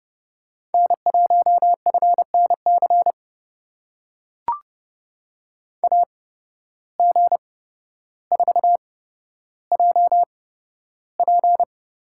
Morse code, heard as a keyed tone, about 30 wpm, spaced as for 13 wpm, 700 Hz. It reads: N1FNC AG4JP
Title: USA Call Signs - 5 Characters - 8x Character Spacing - 30wpm